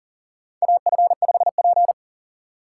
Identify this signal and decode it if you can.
Morse code, heard as a keyed tone, about 40 wpm, 700 Hz. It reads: AF5P
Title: K1USN SST QSO Practice - 40wpm